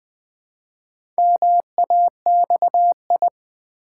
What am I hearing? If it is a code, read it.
Morse code, heard as a keyed tone, about 21 wpm, 700 Hz. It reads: MAXI
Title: Build A Word - 2 Letters At A Time - 8-Letter Words - 20wpm